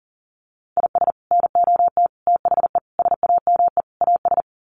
Morse code, heard as a keyed tone, about 37 wpm, 700 Hz. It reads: IS NOT THE SAME AS
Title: Sets of 5 Words Encore 40wpm